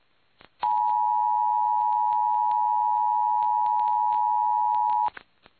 Emergency Alert System (SAME) tones (the 853 and 960 Hz attention tone). It is heard from 0.6 to 5.1 s.